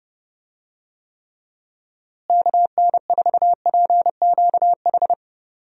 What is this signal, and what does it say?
Morse code, heard as a keyed tone, about 30 wpm, 700 Hz. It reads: KN4PQH
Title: USA Call Signs - 6 Characters - Repeated 2x - 30wpm